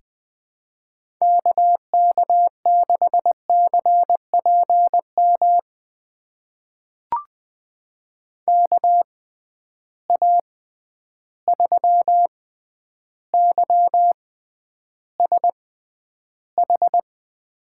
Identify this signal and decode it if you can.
Morse code, heard as a keyed tone, about 20 wpm, spaced as for 10 wpm, 700 Hz. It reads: KK6CPM KA3YSH
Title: USA Call Signs - 6 Characters - 6x Character Spacing - 20wpm